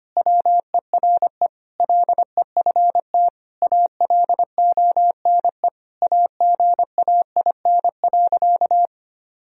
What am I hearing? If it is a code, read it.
Morse code, heard as a keyed tone, about 25 wpm, 700 Hz. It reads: WERE LEFT ALONE AGAIN.
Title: Sentences from Top 1000 Words - Review Part 1 - 25wpm